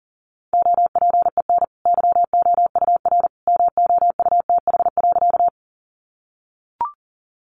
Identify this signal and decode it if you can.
Morse code, heard as a keyed tone, about 40 wpm, 700 Hz. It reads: OPEN YOUR MOUTH.
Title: Sentences from Top 1500 Words 40wpm